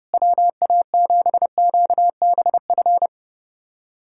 Morse code, heard as a keyed tone, about 30 wpm, 700 Hz. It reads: WA7QBF
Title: USA Callsign DE Callsign 30wpm